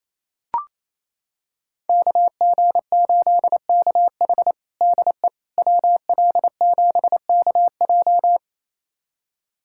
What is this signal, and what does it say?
Morse code, heard as a keyed tone, about 27 wpm, 700 Hz. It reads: KG8KH DE WL7KJ
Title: USA Callsign DE Callsign - 5 Characters - 28wpm